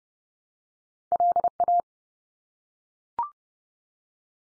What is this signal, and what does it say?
Morse code, heard as a keyed tone, about 30 wpm, 700 Hz. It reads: LA
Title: US States 30WPM